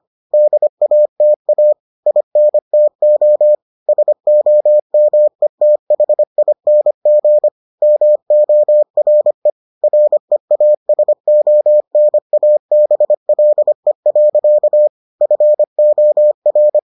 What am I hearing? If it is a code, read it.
Morse code, heard as a keyed tone, about 25 wpm, 600 Hz. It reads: DATA INTO SOMETHING MORE REASONABLE. FOR